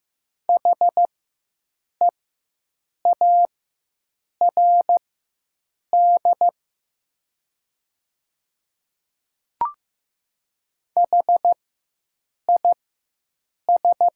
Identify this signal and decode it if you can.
Morse code, heard as a keyed tone, about 15 wpm, spaced as for 7 wpm, 700 Hz. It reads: HEARD HIS
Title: 4x Character Spacing - Top 500 Words - 15wpm